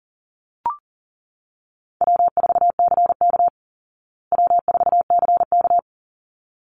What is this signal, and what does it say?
Morse code, heard as a keyed tone, about 37 wpm, 700 Hz. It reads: W4CK W4CK
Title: Easy Call Signs Repeated 2x - 40wpm